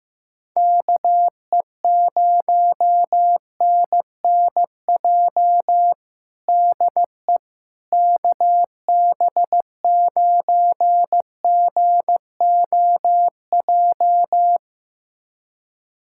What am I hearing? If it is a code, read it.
Morse code, heard as a keyed tone, about 16 wpm, 700 Hz. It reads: KE0NNJ DE KB9GOJ
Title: USA Callsign DE Callsign - 6 Characters - 15wpm